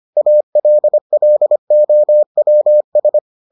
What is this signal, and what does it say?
Morse code, heard as a keyed tone, about 25 wpm, 600 Hz. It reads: ALLOWS